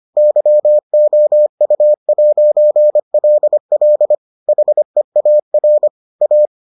Morse code, heard as a keyed tone, about 25 wpm, 600 Hz. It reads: YOU'LL HEAR A